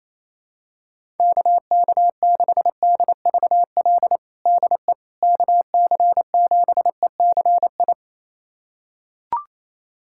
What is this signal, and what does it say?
Morse code, heard as a keyed tone, about 28 wpm, 700 Hz. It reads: KK6DVL DE KC7ECI
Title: USA Callsign DE Callsign - 6 Characters - 28wpm